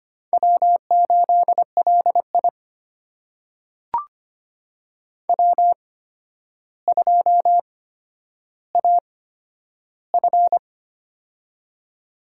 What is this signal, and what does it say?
Morse code, heard as a keyed tone, about 25 wpm, spaced as for 11 wpm, 700 Hz. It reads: W8LI W2AF
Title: USA Call Signs - 4 Characters - 8x Character Spacing - 25wpm